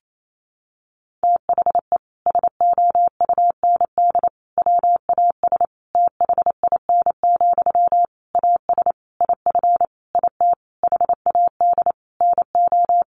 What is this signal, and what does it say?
Morse code, heard as a keyed tone, about 28 wpm, 700 Hz. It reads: THE SOUND WAS THIN, AS IF IT HAD NO
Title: Sentences from Top 1200 Words 28wpm